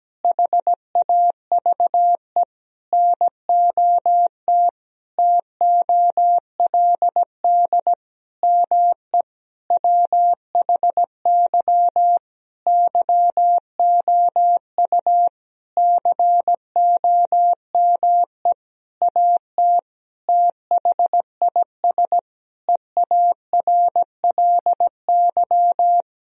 Morse code, heard as a keyed tone, about 17 wpm, 700 Hz. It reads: HAVE NOT TOLD ME WHY YOU COME AT THIS EARLY
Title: Sentences from Top 1000 Words - Review Part 2 - 17wpm